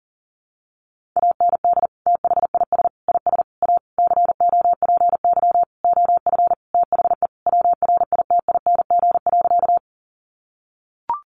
Morse code, heard as a keyed tone, about 37 wpm, 700 Hz. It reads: AND THIS IS A COPY OF THE WRITING.
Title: Sentences from Top 1400 Words 40wpm